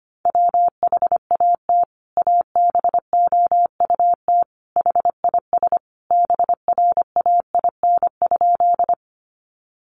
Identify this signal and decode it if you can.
Morse code, heard as a keyed tone, about 25 wpm, 700 Hz. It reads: WHAT ABOUT HIS BRAIN?
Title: Sentences from Top 1100 Words 25wpm